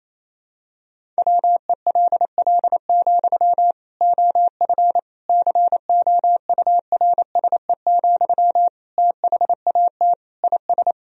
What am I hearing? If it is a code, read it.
Morse code, heard as a keyed tone, about 28 wpm, 700 Hz. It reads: WELL, OF COURSE, THAT IS
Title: Sentences from Top 1000 Words - Review Part 1 - 28wpm